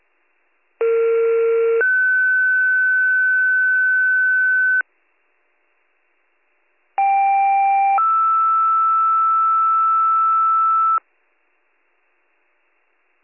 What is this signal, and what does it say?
Two-tone page, heard as a one-second tone, then a three-second tone, about 460 then 1500 Hz, about 790 then 1320 Hz. It is heard from 0.8 to 11.0 s.